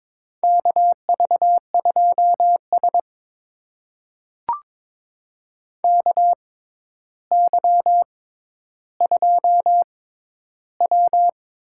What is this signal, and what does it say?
Morse code, heard as a keyed tone, about 21 wpm, spaced as for 11 wpm, 700 Hz. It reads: KV2S KY2W
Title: USA Call Signs - 4 Characters - 6x Character Spacing - 22wpm